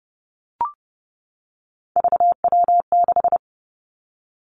Morse code, heard as a keyed tone, about 30 wpm, 700 Hz. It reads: VW6
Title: Three Challenging Letter-Number Combinations 30wpm